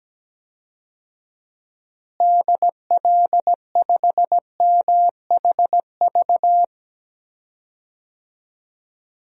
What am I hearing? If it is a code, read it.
Morse code, heard as a keyed tone, about 17 wpm, 700 Hz. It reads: DL5MHV
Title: DXCC-100 Call Signs - 6 Characters - Repeated 2x - 17wpm